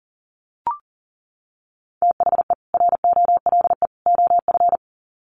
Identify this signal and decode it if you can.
Morse code, heard as a keyed tone, about 38 wpm, 700 Hz. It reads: THE ROLE OF
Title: Sets of 3 Words 40wpm